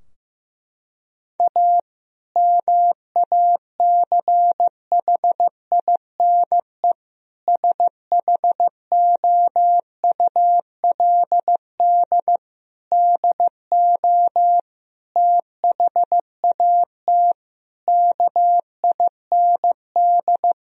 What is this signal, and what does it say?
Morse code, heard as a keyed tone, about 15 wpm, 700 Hz. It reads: A MACHINE SHOULD DO THAT KIND
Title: Sentences from Top 1000 Words - Review Part 2 - 15wpm